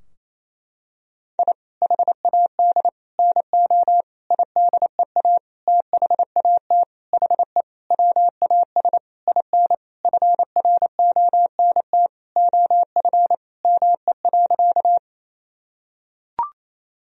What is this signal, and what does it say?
Morse code, heard as a keyed tone, about 28 wpm, 700 Hz. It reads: I HAD NO IDEA THAT HE WAS IN FRONT OF ME.
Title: Sentences from Top 1000 Words - Review Part 2 - 28wpm